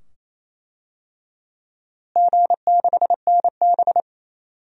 Morse code, heard as a keyed tone, about 28 wpm, 700 Hz. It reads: G6NB
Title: DXCC-100 Call Signs - 4 Characters - Repeated 2x - 28wpm